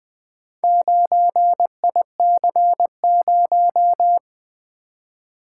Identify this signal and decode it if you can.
Morse code, heard as a keyed tone, about 20 wpm, 700 Hz. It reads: 9IC0
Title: Four Letter-Number Combinations - 20wpm